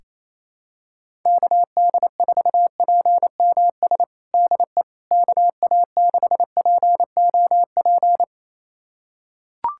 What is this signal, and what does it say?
Morse code, heard as a keyed tone, about 28 wpm, 700 Hz. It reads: KD4PMS DE KA6POP
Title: USA Callsign DE Callsign 28wpm